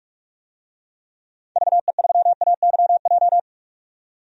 Morse code, heard as a keyed tone, about 41 wpm, 700 Hz. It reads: VE3AYJ
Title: DXCC-100 Call Signs - 6 Characters - 45wpm